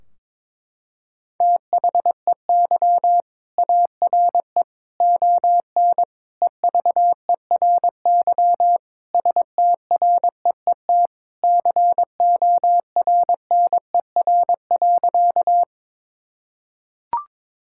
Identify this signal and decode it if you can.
Morse code, heard as a keyed tone, about 21 wpm, 700 Hz. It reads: THEY ARE ON EVERY STREET CORNER.
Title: Sentences from Top 1600 Words 22wpm